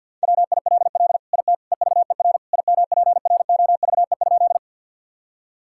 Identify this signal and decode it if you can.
Morse code, heard as a keyed tone, about 48 wpm, 700 Hz. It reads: WILL IT EVER IMPROVE?
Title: Sentences from Top 1000 Words - Review Part 5 - 50wpm